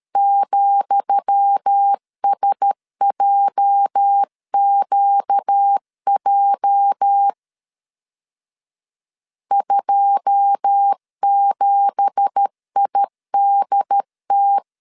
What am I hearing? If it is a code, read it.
Morse code, heard as a keyed tone, about 13 wpm, 800 Hz. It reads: ,SJQJ 27IDT
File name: LEZIONI / 10 SIGNS / 10D COMA 12wpm 2544